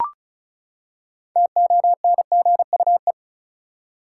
Morse code, heard as a keyed tone, about 34 wpm, 700 Hz. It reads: TONGUE
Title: Six Letter Words - 35wpm